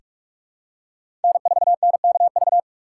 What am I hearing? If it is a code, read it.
Morse code, heard as a keyed tone, about 43 wpm, 700 Hz. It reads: N4NKV
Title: USA Call Signs - 5 Characters - 2x Character Spacing - 45wpm